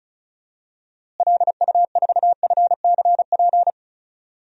Morse code, heard as a keyed tone, about 34 wpm, 700 Hz. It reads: LU4FCP
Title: DXCC-100 Call Signs - 6 Characters - Repeated 2x - 35wpm